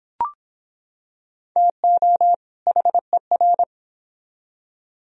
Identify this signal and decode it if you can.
Morse code, heard as a keyed tone, about 25 wpm, 700 Hz. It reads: TO HER